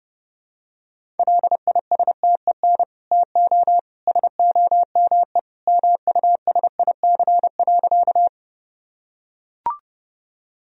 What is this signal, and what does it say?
Morse code, heard as a keyed tone, about 30 wpm, 700 Hz. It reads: LISTEN TO SOME MUSIC.
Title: Sentences from Top 1000 Words - Review Part 1 - 30wpm